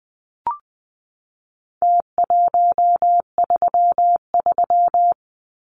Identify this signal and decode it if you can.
Morse code, heard as a keyed tone, about 20 wpm, 700 Hz. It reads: T133
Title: Numbers - Contest Style - 4-Digit 20wpm